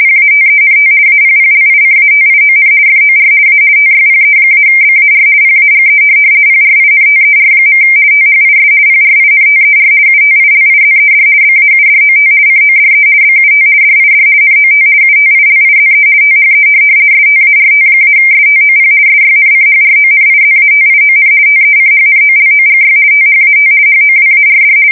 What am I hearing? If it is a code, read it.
RTTY, about 50 baud. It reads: PATENT APPLICATION FOR A TYPE-BAR PAGE PRINTER. THIS PRINTER UTILIZED BAUDOT CODE BUT DID NOT UTILIZE THE START-STOP SYNCHRONIZATION TECHNOLOGY THAT HOWARD
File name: fsk170-50B / teletype-corp